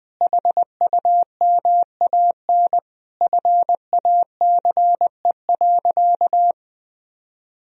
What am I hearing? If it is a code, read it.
Morse code, heard as a keyed tone, about 21 wpm, 700 Hz. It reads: HUMAN FACE.
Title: Sentences from Top 1000 Words - Review Part 3 - 20wpm